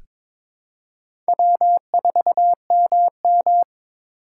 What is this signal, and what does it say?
Morse code, heard as a keyed tone, about 22 wpm, 700 Hz. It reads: W4MM
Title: USA Call Signs - 4 Characters - 2x Character Spacing - 22wpm